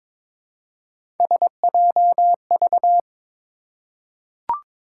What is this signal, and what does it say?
Morse code, heard as a keyed tone, about 22 wpm, 700 Hz. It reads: SJV
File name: ARRL Sections v2 - 22wpm